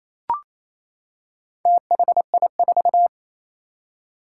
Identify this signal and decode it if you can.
Morse code, heard as a keyed tone, about 27 wpm, 700 Hz. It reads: THI4
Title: Four Challenging Letter-Number Combinations 28wpm